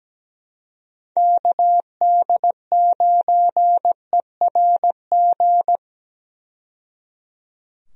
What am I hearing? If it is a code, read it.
Morse code, heard as a keyed tone, about 17 wpm, 700 Hz. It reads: KD9ERG
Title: USA Call Signs - 6 Characters - 17wpm